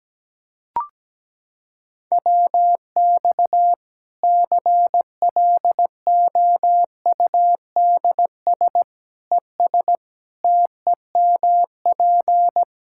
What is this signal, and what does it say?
Morse code, heard as a keyed tone, about 17 wpm, 700 Hz. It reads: WX CLOUDS ES TEMP
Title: K1USN SST QSO Practice 17wpm